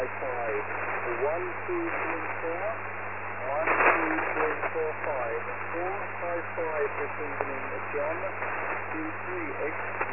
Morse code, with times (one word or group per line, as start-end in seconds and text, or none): none